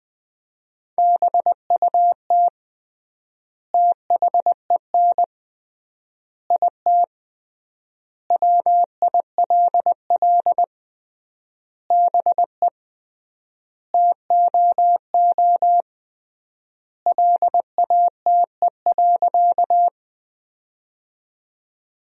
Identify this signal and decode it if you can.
Morse code, heard as a keyed tone, about 20 wpm, 700 Hz. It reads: BUT THEN IT WILL BE TOO LATE.
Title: 3x Word-Spacing - Sentences from Top 300 Words - 20wpm